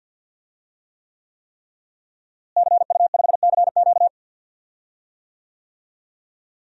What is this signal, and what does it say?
Morse code, heard as a keyed tone, about 52 wpm, 700 Hz. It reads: CU5CQ
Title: DXCC-100 Call Signs - 5 Characters - Repeated 3x - 50wpm